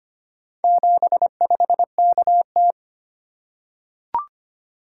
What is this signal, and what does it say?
Morse code, heard as a keyed tone, about 25 wpm, 700 Hz. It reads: 75KT
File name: practice / 25wpm / Four Letter-Number Combinations - 25wpm